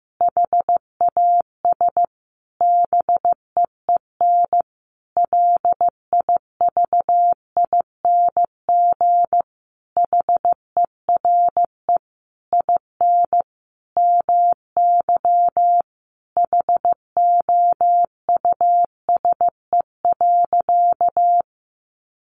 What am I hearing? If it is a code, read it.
Morse code, heard as a keyed tone, about 14 wpm, 700 Hz. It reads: HAS BEEN LIVING HERE IN MY HOUSE.
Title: Sentences from Top 1100 Words 15wpm